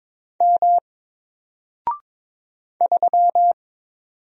Morse code, heard as a keyed tone, about 22 wpm, 700 Hz. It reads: M 3
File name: Single-Letter-Number-22wpm